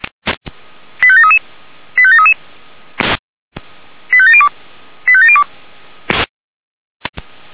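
Five-tone selective call (ZVEI), heard as quick runs of five tones, tones of about 70 ms. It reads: 85733 85733 85702 85702
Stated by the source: ZVEI five-tone call 85385 received on SDR.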